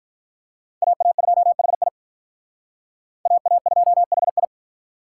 Morse code, heard as a keyed tone, about 45 wpm, 700 Hz. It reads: AA2HI AA2HI